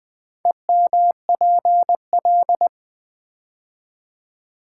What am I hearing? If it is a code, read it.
Morse code, heard as a keyed tone, about 20 wpm, 700 Hz. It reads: EMPL
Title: Build A Word - 4 Letters At A Time - 10-Letter Words - 20wpm